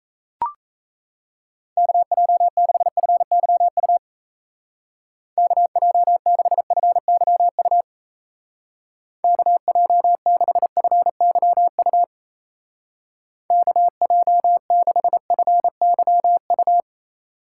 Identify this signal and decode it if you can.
Morse code, heard as a keyed tone, about 34 wpm, 700 Hz. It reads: KJ6FYU KJ6FYU KJ6FYU KJ6FYU
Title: Speed Racing - USA 2x3 Call Signs 28wpm